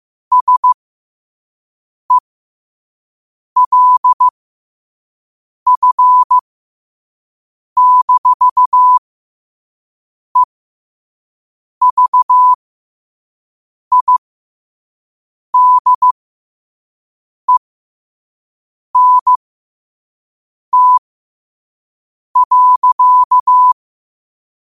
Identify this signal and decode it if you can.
Morse code, heard as a keyed tone, about 15 wpm, spaced as for 5 wpm, 1000 Hz. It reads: SELF-EVIDENT.
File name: qotdfiles / 5wpm